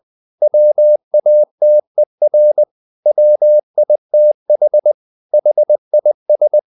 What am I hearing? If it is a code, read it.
Morse code, heard as a keyed tone, about 20 wpm, 600 Hz. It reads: WATER WITH HIS
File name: the old man and the sea / the old man and the sea part 1 20wpm